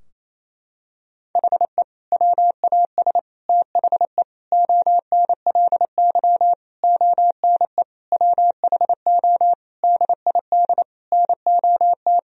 Morse code, heard as a keyed tone, about 27 wpm, 700 Hz. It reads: HE WAS THE ONLY ONE WHO DID NOT
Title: Sentences from Top 1000 Words - Review Part 3 - 28wpm